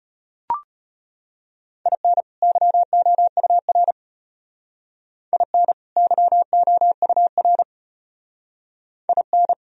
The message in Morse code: IN YOUR IN YOUR IN

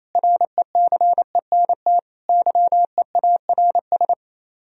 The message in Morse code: RECENT YEARS

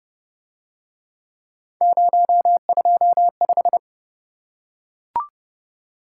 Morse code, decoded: 025

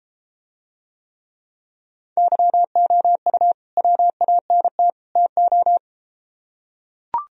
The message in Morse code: YOU WANT TO

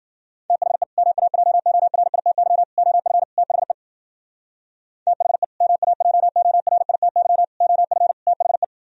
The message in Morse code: THE MAJORITY OF THE THE MAJORITY OF THE

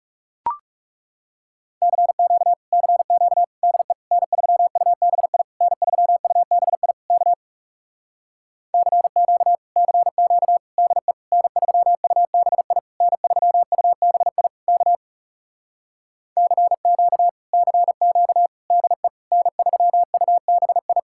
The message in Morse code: CQ CQ DE N3UBI N3UBI K CQ CQ DE N3UBI N3UBI K CQ CQ DE N3UBI

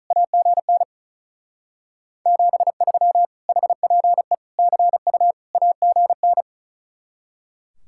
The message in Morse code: AGN 73 HPE CU AGN

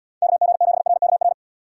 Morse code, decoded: KG7RKR

